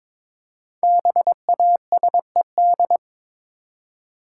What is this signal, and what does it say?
Morse code, heard as a keyed tone, about 23 wpm, 700 Hz. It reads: BASED
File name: 5 Letter Words - First 3 Letters Given - 22wpm